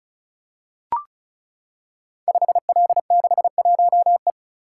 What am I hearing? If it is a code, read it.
Morse code, heard as a keyed tone, about 36 wpm, 700 Hz. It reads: 5L61E
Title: Code Groups - 5 Letter:Numbers - 35wpm